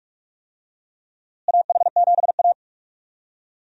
Morse code, heard as a keyed tone, about 46 wpm, 700 Hz. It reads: AH7A
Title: USA Call Signs - 4 Characters - Repeated 2x - 45wpm